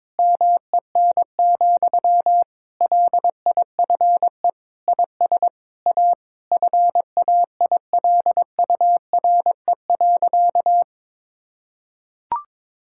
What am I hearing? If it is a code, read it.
Morse code, heard as a keyed tone, about 22 wpm, 700 Hz. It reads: MEN, LIFE IS A FAILURE.
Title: Sentences from Top 1000 Words - Review Part 6 - 22wpm